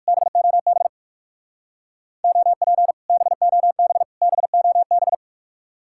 Morse code, heard as a keyed tone, about 57 wpm, 700 Hz. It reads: BOB OP BOB BOB